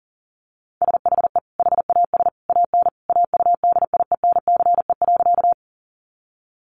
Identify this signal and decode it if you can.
Morse code, heard as a keyed tone, about 40 wpm, 700 Hz. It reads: SHE HAS AN AUDIENCE.
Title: Sentences from Top 800 Words 40wpm